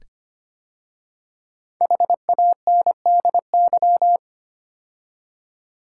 Morse code, heard as a keyed tone, about 25 wpm, 700 Hz. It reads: HANDY
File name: 5 Letter Words - First 3 Letters Given - 25wpm